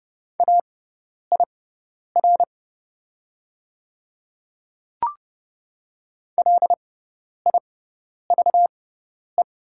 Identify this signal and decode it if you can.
Morse code, heard as a keyed tone, about 30 wpm, spaced as for 10 wpm, 700 Hz. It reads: AIR LIVE
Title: 6x Character Spacing - Top 200 Words - 30wpm